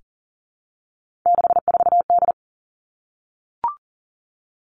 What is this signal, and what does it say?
Morse code, heard as a keyed tone, about 40 wpm, 700 Hz. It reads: 64D
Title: Three Challenging Letter-Number Combinations 40wpm